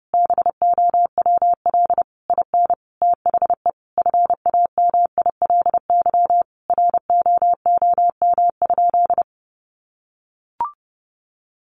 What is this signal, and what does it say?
Morse code, heard as a keyed tone, about 30 wpm, 700 Hz. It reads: BOWL IN THE FAMILY ROOM?
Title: Sentences from Top 1200 Words 30wpm